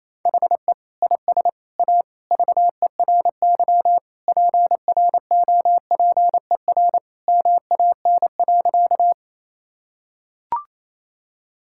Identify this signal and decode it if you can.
Morse code, heard as a keyed tone, about 28 wpm, 700 Hz. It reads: HE IS A VERY PROPER MAN.
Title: Sentences from Top 1000 Words - Review Part 5 - 28wpm